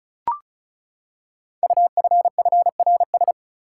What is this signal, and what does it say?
Morse code, heard as a keyed tone, about 36 wpm, 700 Hz. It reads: UFFRS
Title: Code Groups - 5 Letters - 35wpm